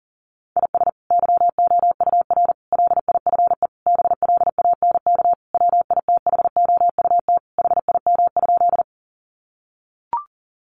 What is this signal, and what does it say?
Morse code, heard as a keyed tone, about 40 wpm, 700 Hz. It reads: IS YOUR LIFE BLANK WITHOUT HIM?
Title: Sentences from Top 1500 Words 40wpm